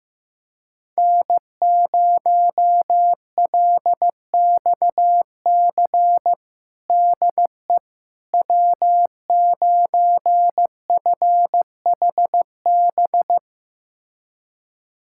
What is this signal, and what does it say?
Morse code, heard as a keyed tone, about 15 wpm, 700 Hz. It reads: N0LXC DE W9FHB
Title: USA Callsign DE Callsign - 5 Characters - 15wpm